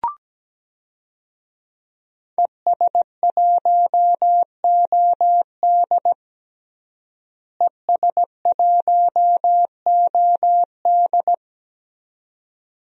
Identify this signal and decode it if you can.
Morse code, heard as a keyed tone, about 17 wpm, 700 Hz. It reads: ES1OD ES1OD
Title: DXCC-100 Call Signs - 5 Characters - Repeated 2x - 17wpm